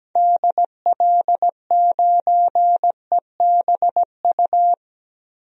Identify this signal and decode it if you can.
Morse code, heard as a keyed tone, about 17 wpm, 700 Hz. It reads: DL9EBU
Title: DXCC-100 Call Signs - 6 Characters - 17wpm